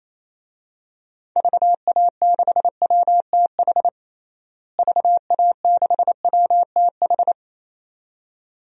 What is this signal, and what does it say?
Morse code, heard as a keyed tone, about 28 wpm, 700 Hz. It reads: VA6WTH VA6WTH
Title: DXCC-100 Call Signs - 6 Characters - Repeated 2x - 28wpm